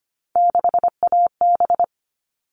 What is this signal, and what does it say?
Morse code, heard as a keyed tone, about 25 wpm, 700 Hz. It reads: 6AB
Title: All Cut Numbers - 3-Digit 25wpm